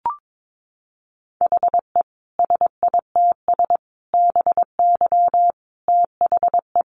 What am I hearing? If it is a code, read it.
Morse code, heard as a keyed tone, about 22 wpm, 700 Hz. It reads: HE SITS BY THE
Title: Easy Phrases 22wpm